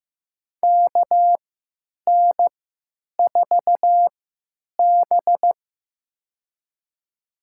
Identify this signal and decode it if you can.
Morse code, heard as a keyed tone, about 15 wpm, spaced as for 8 wpm, 700 Hz. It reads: KN4B
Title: USA Call Signs - 3x Character Spacing - 15wpm